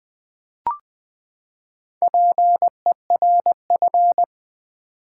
Morse code, heard as a keyed tone, about 20 wpm, 700 Hz. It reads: PERF
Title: Build A Word - 4 Letters At A Time - 11-Letter Words - 20wpm